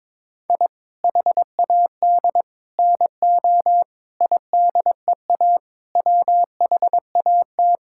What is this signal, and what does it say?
Morse code, heard as a keyed tone, about 21 wpm, 700 Hz. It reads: I HAD NO IDEA WHAT
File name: practice / 22wpm / Sets of 8 Words - 22wpm